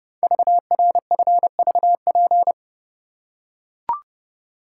Morse code, heard as a keyed tone, about 30 wpm, 700 Hz. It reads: VRFVP
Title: Code Groups - 5 Letters - 30wpm